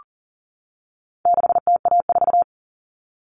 Morse code, heard as a keyed tone, about 37 wpm, 700 Hz. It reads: 6TA4
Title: All Cut Numbers - 4-Digit 40wpm